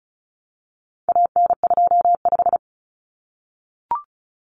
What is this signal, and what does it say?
Morse code, heard as a keyed tone, about 34 wpm, 700 Hz. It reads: AN25
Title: Common Cut Numbers - 4-Digit 35wpm